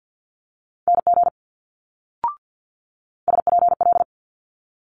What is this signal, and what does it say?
Morse code, heard as a keyed tone, about 50 wpm, 700 Hz. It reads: NG SPL
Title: Multi-letter Phonemes Quick Fire 50wpm